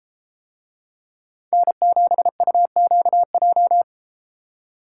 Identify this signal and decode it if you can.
Morse code, heard as a keyed tone, about 32 wpm, 700 Hz. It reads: N7UQJ